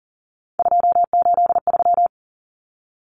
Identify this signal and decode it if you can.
Morse code, heard as a keyed tone, about 40 wpm, 700 Hz. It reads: 283
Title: Common Cut Numbers - 3-Digit 40wpm